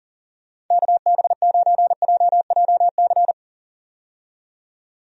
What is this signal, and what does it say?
Morse code, heard as a keyed tone, about 38 wpm, 700 Hz. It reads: KB9JJC
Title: USA Call Signs - 6 Characters - 40wpm